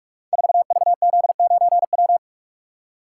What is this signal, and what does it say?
Morse code, heard as a keyed tone, about 44 wpm, 700 Hz. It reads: 4VZ9W
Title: Code Groups - 5 Letter:Numbers - 45wpm